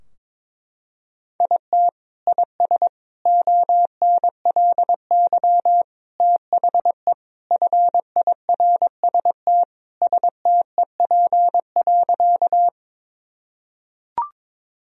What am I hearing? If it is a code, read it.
Morse code, heard as a keyed tone, about 21 wpm, 700 Hz. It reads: IT IS ONLY THE FIRST STEP.
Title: Sentences from Top 1000 Words - Review Part 2 - 22wpm